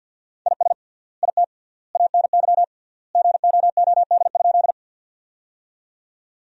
Morse code, heard as a keyed tone, about 50 wpm, 700 Hz. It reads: IS IT ANY GOOD?
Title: 3x Word-Spacing - Sentences from Top 200 Words - 50wpm